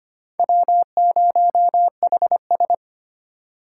Morse code, heard as a keyed tone, about 32 wpm, 700 Hz. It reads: W0HS